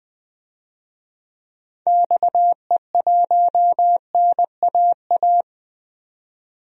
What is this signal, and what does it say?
Morse code, heard as a keyed tone, about 20 wpm, 700 Hz. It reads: XE1NAA
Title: DXCC-100 Call Signs - 6 Characters - 20wpm